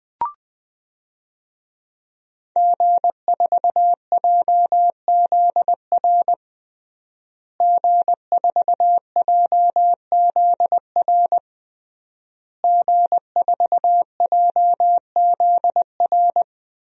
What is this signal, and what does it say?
Morse code, heard as a keyed tone, about 20 wpm, 700 Hz. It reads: G4JZR G4JZR G4JZR
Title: DXCC-100 Call Signs - 5 Characters - Repeated 3x - 20wpm